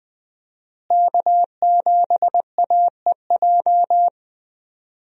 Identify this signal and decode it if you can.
Morse code, heard as a keyed tone, about 20 wpm, 700 Hz. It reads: K7AEJ
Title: USA Call Signs - 5 Characters - Repeated 2x - 20wpm